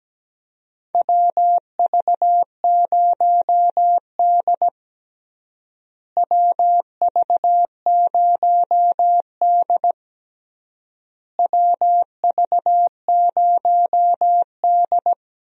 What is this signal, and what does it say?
Morse code, heard as a keyed tone, about 17 wpm, 700 Hz. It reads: WV0D WV0D WV0D
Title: USA Call Signs - 4 Characters - Repeated 3x - 17wpm